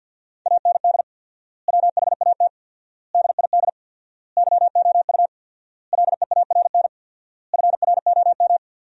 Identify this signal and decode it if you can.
Morse code, heard as a keyed tone, about 50 wpm, 700 Hz. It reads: AND WHAT DID YOU LEARN FROM
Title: 4x Word-Spacing - Sentences from Top 300 Words - 50wpm